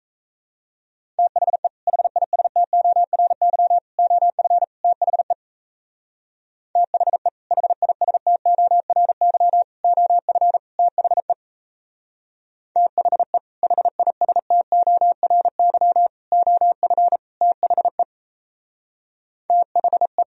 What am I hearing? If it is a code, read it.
Morse code, heard as a keyed tone, about 35 wpm, 700 Hz. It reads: THE HISTORY OF THE THE HISTORY OF THE THE HISTORY OF THE THE